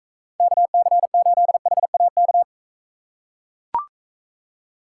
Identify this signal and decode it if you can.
Morse code, heard as a keyed tone, about 41 wpm, 700 Hz. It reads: KC8HAK